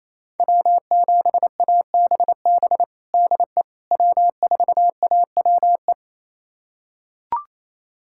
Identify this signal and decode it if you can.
Morse code, heard as a keyed tone, about 28 wpm, 700 Hz. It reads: W7ABB DE W4AWE
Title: USA Callsign DE Callsign - 5 Characters - 28wpm